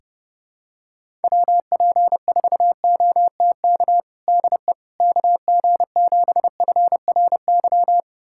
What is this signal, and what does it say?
Morse code, heard as a keyed tone, about 32 wpm, 700 Hz. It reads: WP4OTK DE KG7FRY
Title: USA Callsign DE Callsign - 6 Characters - 30wpm